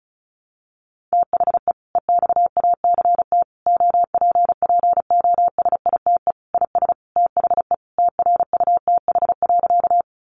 Morse code, heard as a keyed tone, about 34 wpm, 700 Hz. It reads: THE EXACT OPPOSITE IS THE TRUTH.
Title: Sentences from Top 1300 Words 35wpm